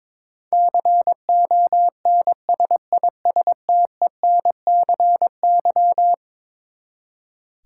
Morse code, heard as a keyed tone, about 22 wpm, 700 Hz. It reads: CONSISTENCY
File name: practice / 22wpm / Eleven Letter Words - 22wpm